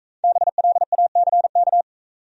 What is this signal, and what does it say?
Morse code, heard as a keyed tone, about 40 wpm, 700 Hz. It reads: BLACK